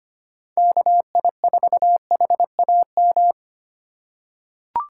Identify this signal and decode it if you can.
Morse code, heard as a keyed tone, about 25 wpm, 700 Hz. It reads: KI4HAM
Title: USA Call Signs - 6 Characters - 2x Character Spacing - 25wpm